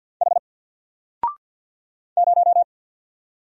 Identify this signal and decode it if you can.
Morse code, heard as a keyed tone, about 50 wpm, 700 Hz. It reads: H 0
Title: Single Letter-Number 50wpm